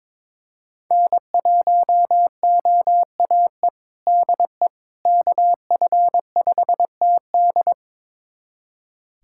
Morse code, heard as a keyed tone, about 22 wpm, 700 Hz. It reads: N1OAE DE KF5TD
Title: USA Callsign DE Callsign - 5 Characters - 22wpm